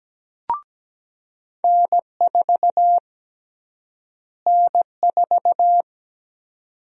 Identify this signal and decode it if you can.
Morse code, heard as a keyed tone, about 17 wpm, 700 Hz. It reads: N4 N4
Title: USA Call Sign Prefixes - Repeated 2x - 17wpm